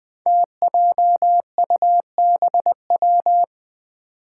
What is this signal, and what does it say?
Morse code, heard as a keyed tone, about 20 wpm, 700 Hz. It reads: TJUBW
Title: Five Challenging Letter-Number Combinations 20wpm